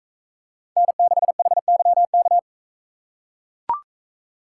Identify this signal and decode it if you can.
Morse code, heard as a keyed tone, about 44 wpm, 700 Hz. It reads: N6HYK